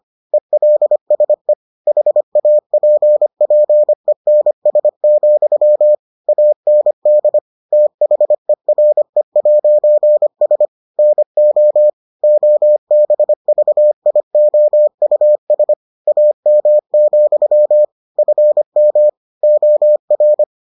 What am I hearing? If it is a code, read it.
Morse code, heard as a keyed tone, about 25 wpm, 600 Hz. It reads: ELSE HAPPENS, AND THERE'S NO OBVIOUS AM, FM OR